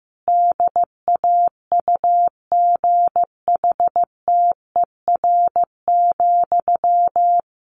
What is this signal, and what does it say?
Morse code, heard as a keyed tone, about 15 wpm, 700 Hz. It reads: DAUGHTER,
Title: Sentences from Top 1500 Words 15wpm